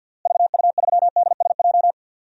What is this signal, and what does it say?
Morse code, heard as a keyed tone, about 52 wpm, 700 Hz. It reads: VU3DSJ
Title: DXCC-100 Call Signs - 6 Characters - Repeated 3x - 50wpm